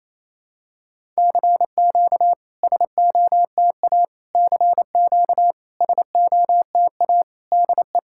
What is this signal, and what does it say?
Morse code, heard as a keyed tone, about 28 wpm, 690 Hz. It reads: CQ SOTA CQ SOTA DE
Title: SOTA QSO Practice 28wpm